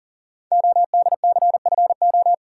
Morse code, heard as a keyed tone, about 39 wpm, 700 Hz. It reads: ODCFO
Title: Code Groups - 5 Letters - 40wpm